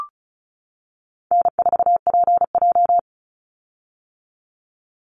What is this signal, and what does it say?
Morse code, heard as a keyed tone, about 34 wpm, 700 Hz. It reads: N4PJ
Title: Call Signs - 35wpm